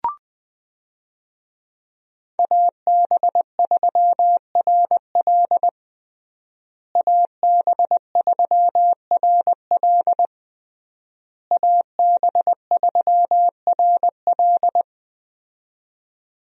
Morse code, heard as a keyed tone, about 21 wpm, 700 Hz. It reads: AB3RL AB3RL AB3RL
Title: USA Call Signs - 5 Characters - Repeated 3x - 20wpm